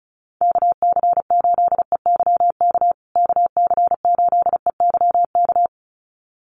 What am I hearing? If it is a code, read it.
Morse code, heard as a keyed tone, about 34 wpm, 700 Hz. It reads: KC8EYK KC8EYK
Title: USA Call Signs Repeated 2x 35wpm